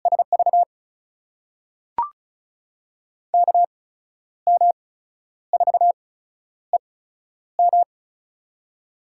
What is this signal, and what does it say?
Morse code, heard as a keyed tone, about 36 wpm, spaced as for 11 wpm, 700 Hz. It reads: SV KM4EM
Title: USA Call Signs - 5 Characters - 8x Character Spacing - 35wpm